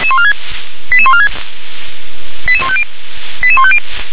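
Five-tone selective call (ZVEI), heard as quick runs of five tones, tones of about 80 ms. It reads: 88146 88150 88150
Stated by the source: ZVEI five-tone selective call received by SDR.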